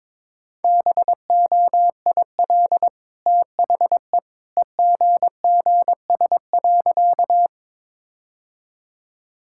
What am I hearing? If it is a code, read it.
Morse code, heard as a keyed tone, about 22 wpm, 690 Hz. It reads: BOIL THE EGGS.
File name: practice / 22wpm / Easy Sentences with Hints - 22wpm